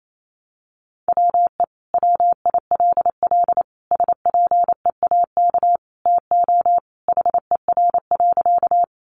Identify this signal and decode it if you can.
Morse code, heard as a keyed tone, about 28 wpm, 700 Hz. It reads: WE WILL SPEAK TO HER.
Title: Sentences from Top 1100 Words 28wpm